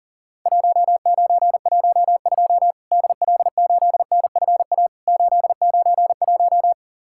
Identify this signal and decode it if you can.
Morse code, heard as a keyed tone, about 40 wpm, 700 Hz. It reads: 1912 DL8NFA 891
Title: Worked All Europe DX Contest - QTC Exchange - 40wpm